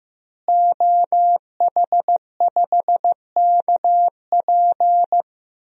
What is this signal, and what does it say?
Morse code, heard as a keyed tone, about 15 wpm, 700 Hz. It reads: OH5KP
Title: DXCC-100 Call Signs - 5 Characters - Repeated 2x - 15wpm